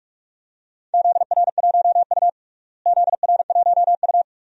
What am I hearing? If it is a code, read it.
Morse code, heard as a keyed tone, about 46 wpm, 700 Hz. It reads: ZR1U ZR1U
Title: DXCC-100 Call Signs - 4 Characters - Repeated 2x - 45wpm